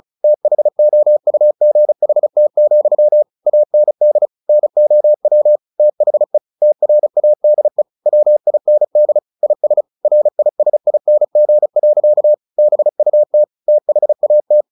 Morse code, heard as a keyed tone, about 34 wpm, 600 Hz. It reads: THOUGHT, AND NOW THE TRADE WIND IS RISING. BUT THAT
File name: the old man and the sea 2 35wpm